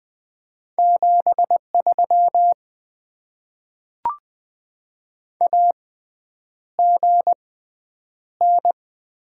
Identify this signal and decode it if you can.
Morse code, heard as a keyed tone, about 20 wpm, spaced as for 9 wpm, 690 Hz. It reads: 73 AGN
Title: Top 20 QSO Elements - 6x Character Spacing - 20wpm